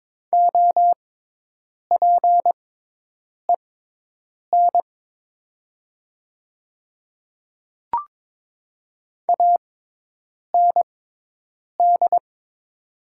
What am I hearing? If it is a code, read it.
Morse code, heard as a keyed tone, about 22 wpm, spaced as for 8 wpm, 700 Hz. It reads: OPEN AND
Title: 6x Character Spacing - Top 300 Words - 22wpm